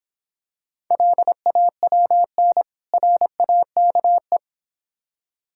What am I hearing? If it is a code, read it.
Morse code, heard as a keyed tone, about 26 wpm, 700 Hz. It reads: LAWN RAKE